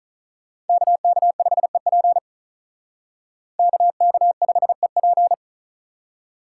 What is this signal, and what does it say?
Morse code, heard as a keyed tone, about 46 wpm, 700 Hz. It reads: KK5EP KK5EP